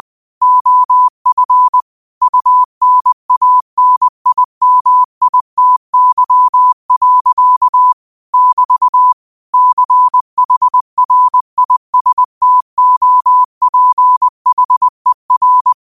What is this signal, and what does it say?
Morse code, heard as a keyed tone, about 20 wpm, 1000 Hz. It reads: OF UNANIMITY. <BT> CHRISTOPHER
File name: qotdfiles / 20wpm